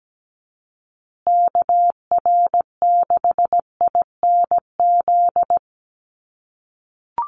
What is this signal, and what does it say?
Morse code, heard as a keyed tone, about 17 wpm, 700 Hz. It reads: KR6INZ
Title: USA Call Signs Repeated 2x 17wpm